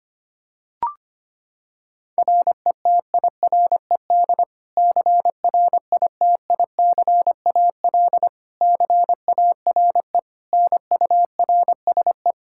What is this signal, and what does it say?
Morse code, heard as a keyed tone, about 25 wpm, 700 Hz. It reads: RETIRED CRITICAL CARE NURSE
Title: Practice QSO - Exchange 3 - More Info - 25wpm